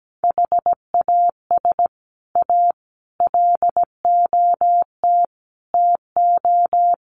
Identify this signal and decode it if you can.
Morse code, heard as a keyed tone, about 17 wpm, 700 Hz. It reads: HAS A LOT TO
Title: Sets of 5 Words Encore2 17wpm